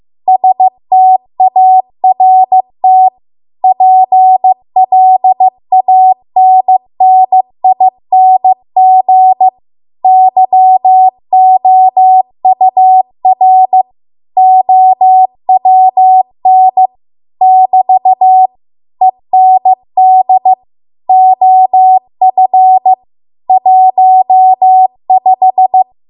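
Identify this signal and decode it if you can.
Morse code, heard as a keyed tone, about 15 wpm, 750 Hz. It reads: START PLANNING YOUR OWN <BT> END OF 15
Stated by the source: Morse code practice at 15 wpm from the national amateur radio association's code archive.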